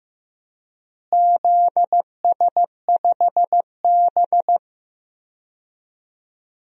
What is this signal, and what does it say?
Morse code, heard as a keyed tone, about 15 wpm, 700 Hz. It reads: ZS5B
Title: DXCC-100 Call Signs - 4 Characters - Repeated 2x - 15wpm